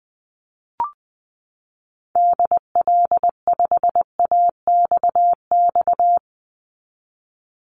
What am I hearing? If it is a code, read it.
Morse code, heard as a keyed tone, about 20 wpm, 700 Hz. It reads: DL5AXX
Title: Call Signs - 20wpm